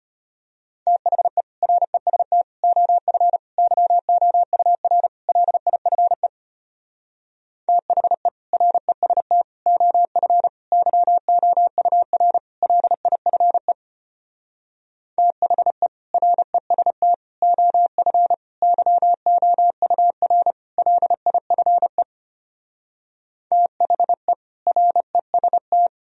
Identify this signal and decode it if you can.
Morse code, heard as a keyed tone, about 33 wpm, 700 Hz. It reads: THE REST OF YOUR LIFE THE REST OF YOUR LIFE THE REST OF YOUR LIFE THE REST